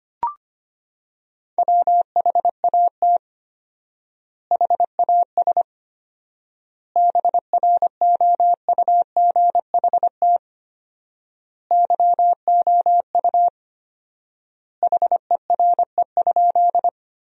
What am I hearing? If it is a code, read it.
Morse code, heard as a keyed tone, about 25 wpm, 700 Hz. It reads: WHAT HAS BROUGHT YOU HERE?
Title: 4x Word-Spacing - Sentences from Top 500 Words - 25wpm